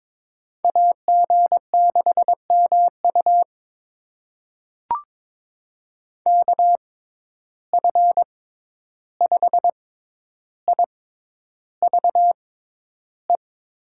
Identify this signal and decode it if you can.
Morse code, heard as a keyed tone, about 22 wpm, spaced as for 11 wpm, 700 Hz. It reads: AG6MU KF5IVE
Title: USA Call Signs - 6x Character Spacing - 22wpm